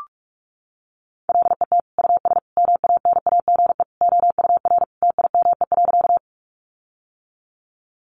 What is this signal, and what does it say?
Morse code, heard as a keyed tone, about 45 wpm, 700 Hz. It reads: LET US MANAGE OUR TIME.
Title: Sentences from Top 1400 Words 45wpm